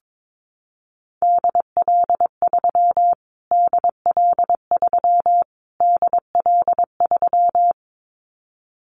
Morse code, heard as a keyed tone, about 22 wpm, 700 Hz. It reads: DL3 DL3 DL3
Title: Callsign Prefixes for Top 25 DXCC - Repeated 3x - 22wpm